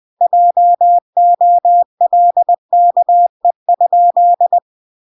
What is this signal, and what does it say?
Morse code, heard as a keyed tone, about 20 wpm, 700 Hz. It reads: JOLKE?
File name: recordings / 20wpm0000